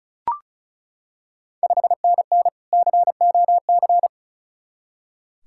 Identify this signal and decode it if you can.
Morse code, heard as a keyed tone, about 36 wpm, 690 Hz. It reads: 5NN COC
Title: Ontario QSO Party Exchange 35wpm